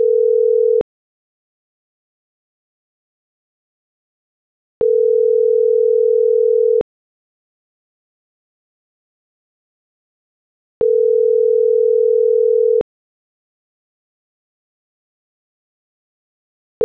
Ringback tone (North America), 440 + 480 Hz.
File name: CENTRAL MAERSK / out andres.rdzmsk